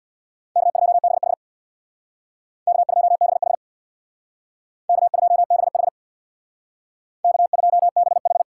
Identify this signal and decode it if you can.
Morse code, heard as a keyed tone, about 62 wpm, 700 Hz. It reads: K2BH K2BH K2BH K2BH
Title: Speed Racing - USA 1x2 Call Signs 50wpm